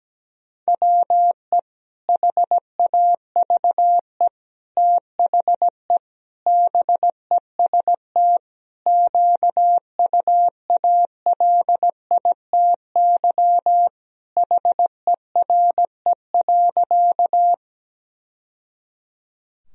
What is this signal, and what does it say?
Morse code, heard as a keyed tone, about 17 wpm, 700 Hz. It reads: WE HAVE THE BEST QUALITY HERE.
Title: Sentences from Top 1000 Words - Review Part 4 - 17wpm